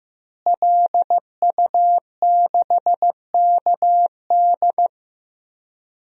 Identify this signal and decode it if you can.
Morse code, heard as a keyed tone, about 15 wpm, 700 Hz. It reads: LU6KD
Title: DXCC-100 Call Signs - 5 Characters - Repeated 2x - 15wpm